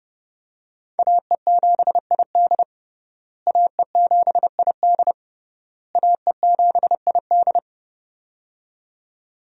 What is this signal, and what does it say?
Morse code, heard as a keyed tone, about 30 wpm, 700 Hz. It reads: AE7ID AE7ID AE7ID
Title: USA Call Signs - 5 Characters - Repeated 3x - 30wpm